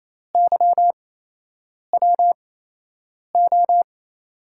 Morse code, heard as a keyed tone, about 27 wpm, spaced as for 8 wpm, 700 Hz. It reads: YWO
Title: USA Call Signs - 8x Character Spacing - 28wpm